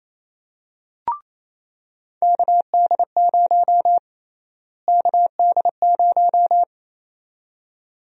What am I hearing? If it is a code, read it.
Morse code, heard as a keyed tone, about 28 wpm, 700 Hz. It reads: KD0 KD0
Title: USA Call Sign Prefixes - Repeated 2x - 28wpm